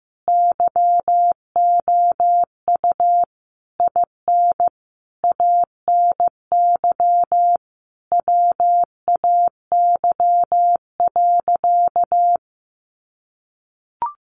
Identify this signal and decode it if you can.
Morse code, heard as a keyed tone, about 15 wpm, 700 Hz. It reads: YOU IN ANY WAY.
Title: Sentences from Top 1100 Words 15wpm